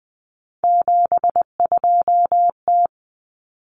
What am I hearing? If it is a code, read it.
Morse code, heard as a keyed tone, about 20 wpm, 700 Hz. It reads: 72T
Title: Common Cut Numbers - 3-Digit 20wpm